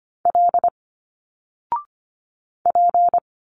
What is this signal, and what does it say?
Morse code, heard as a keyed tone, about 25 wpm, 700 Hz. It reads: L P
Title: Single Letters 25wpm